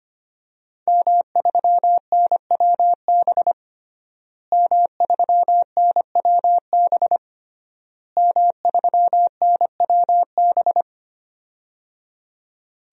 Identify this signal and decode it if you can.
Morse code, heard as a keyed tone, about 25 wpm, 700 Hz. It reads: M3NWB M3NWB M3NWB
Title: DXCC-100 Call Signs - 5 Characters - Repeated 3x - 25wpm